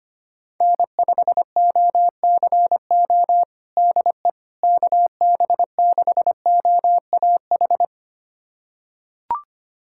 Morse code, heard as a keyed tone, about 25 wpm, 700 Hz. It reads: N5OCO DE KB6OAH
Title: USA Callsign DE Callsign 25wpm